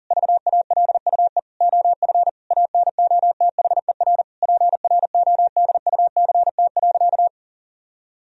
Morse code, heard as a keyed tone, about 40 wpm, 700 Hz. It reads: VALUE OF ANOTHER PRODUCT.
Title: Sentences from Top 1000 Words - Review Part 3 - 40wpm